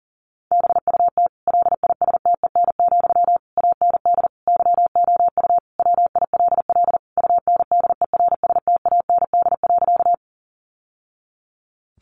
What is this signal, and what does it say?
Morse code, heard as a keyed tone, about 40 wpm, 700 Hz. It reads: BUT LISTEN, AND YOU WILL UNDERSTAND.
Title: Sentences from Top 600 Words 40wpm